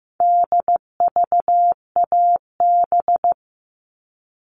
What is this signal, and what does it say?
Morse code, heard as a keyed tone, about 15 wpm, 700 Hz. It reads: DVAB
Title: All Cut Numbers - 4-Digit 15wpm